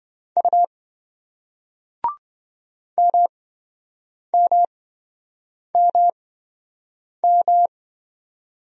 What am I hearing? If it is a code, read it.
Morse code, heard as a keyed tone, about 27 wpm, spaced as for 7 wpm, 700 Hz. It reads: U MMMM